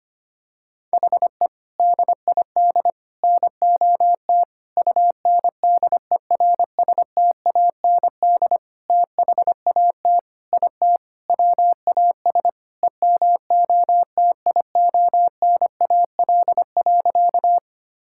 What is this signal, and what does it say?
Morse code, heard as a keyed tone, about 25 wpm, 700 Hz. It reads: HE DID NOT UNDERSTAND THAT IT WAS EMOTIONAL.
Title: Sentences from Top 1000 Words - Review Part 5 - 25wpm